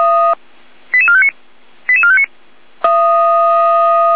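Five-tone selective call (ZVEI), heard as quick runs of five tones, tones of about 70 ms. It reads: 88459 88459
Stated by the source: ZVEI five-tone selective call received on an SDR.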